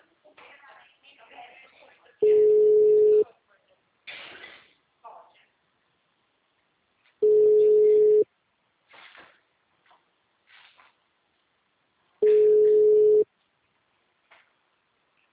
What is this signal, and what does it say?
Ringback tone (Europe, 425 Hz), with speech.
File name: out- -2003- - - .45767